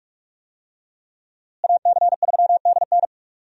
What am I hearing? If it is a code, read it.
Morse code, heard as a keyed tone, about 44 wpm, 700 Hz. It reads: AC3DN
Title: USA Call Signs - 5 Characters - Repeated 3x - 45wpm